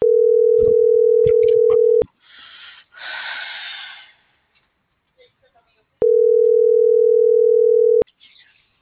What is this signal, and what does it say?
Ringback tone (North America), 440 + 480 Hz.
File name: CENTRAL GOMSA / out lizeth.morales